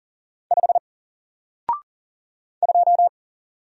Morse code, heard as a keyed tone, about 40 wpm, 700 Hz. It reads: 5 2
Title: Single Character (Letter:Number:Punctuation:Prosign) Alt 40wpm